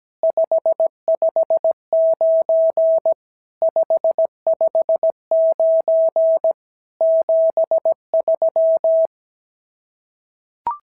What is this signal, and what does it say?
Morse code, heard as a keyed tone, about 17 wpm, 650 Hz. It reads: 559 559 73
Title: SOTA QSO Practice 17wpm